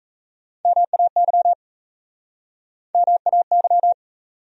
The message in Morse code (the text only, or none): MAY MAY